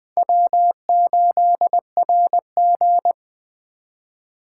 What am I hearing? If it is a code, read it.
Morse code, heard as a keyed tone, about 21 wpm, 700 Hz. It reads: W8RG
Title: USA Call Signs - 3x Character Spacing - 20wpm